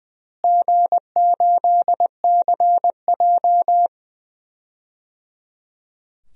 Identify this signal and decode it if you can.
Morse code, heard as a keyed tone, about 20 wpm, 700 Hz. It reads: G8CJ
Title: DXCC-100 Call Signs - 4 Characters - Repeated 3x - 20wpm